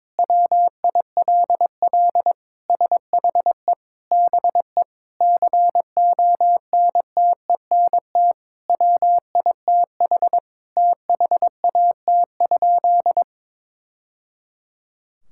Morse code, heard as a keyed tone, about 22 wpm, 700 Hz. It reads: WILL SHE BE CONTENT WITH THAT?
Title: Sentences from Top 1000 Words - Review Part 6 - 22wpm